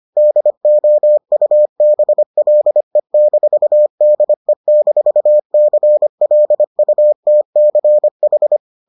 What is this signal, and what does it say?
Morse code, heard as a keyed tone, about 25 wpm, 600 Hz. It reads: DOUBLE-DE-CLUTCH